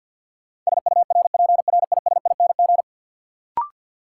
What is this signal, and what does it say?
Morse code, heard as a keyed tone, about 53 wpm, 700 Hz. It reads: SURPRISING
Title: Sets of 5 Words - New Word Drill - 50wpm